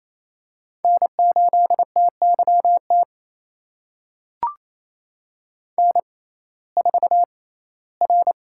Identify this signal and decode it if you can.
Morse code, heard as a keyed tone, about 28 wpm, spaced as for 17 wpm, 700 Hz. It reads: N8TYT N4R
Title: USA Call Signs - 5 Characters - 6x Character Spacing - 28wpm